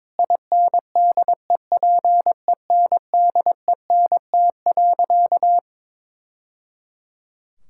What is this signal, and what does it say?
Morse code, heard as a keyed tone, about 22 wpm, 700 Hz. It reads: INDEPENDENT.
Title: Sentences from Top 1000 Words - Review Part 6 - 22wpm